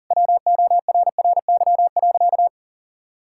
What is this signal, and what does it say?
Morse code, heard as a keyed tone, about 40 wpm, 700 Hz. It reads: WORRY.
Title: Pangram Challenge v2 - 40wpm